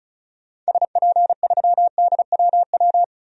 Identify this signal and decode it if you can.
Morse code, heard as a keyed tone, about 34 wpm, 700 Hz. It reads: SP3DWW
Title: DXCC-100 Call Signs - 6 Characters - 35wpm